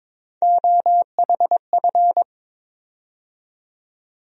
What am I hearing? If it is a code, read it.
Morse code, heard as a keyed tone, about 22 wpm, 700 Hz. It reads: OHF
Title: USA Call Sign Postfixes - 3 Letters 22wpm